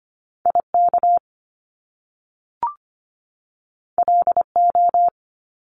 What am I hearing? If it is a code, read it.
Morse code, heard as a keyed tone, about 25 wpm, 700 Hz. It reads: IK LO
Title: Top Two Letter Combinations in Words 25wpm